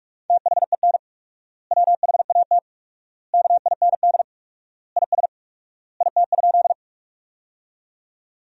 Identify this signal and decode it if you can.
Morse code, heard as a keyed tone, about 41 wpm, 700 Hz. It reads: THEN WHAT KIND IS IT?
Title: 4x Word-Spacing - Sentences from Top 200 Words - 45wpm